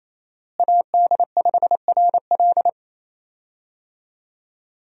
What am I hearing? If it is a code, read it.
Morse code, heard as a keyed tone, about 28 wpm, 700 Hz. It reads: AD5RL
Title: USA Call Signs - 5 Characters - Repeated 3x - 28wpm